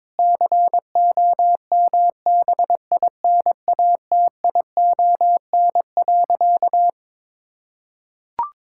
Morse code, heard as a keyed tone, about 22 wpm, 700 Hz. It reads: COMBINATION.
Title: Sentences from Top 1000 Words - Review Part 6 - 22wpm